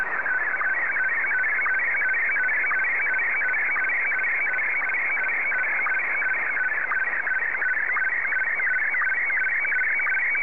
SSTV (slow-scan television), Scottie DX.